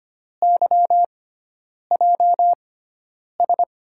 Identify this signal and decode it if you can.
Morse code, heard as a keyed tone, about 25 wpm, spaced as for 9 wpm, 700 Hz. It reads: YJS
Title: USA Call Signs - 6 Characters - 6x Character Spacing - 25wpm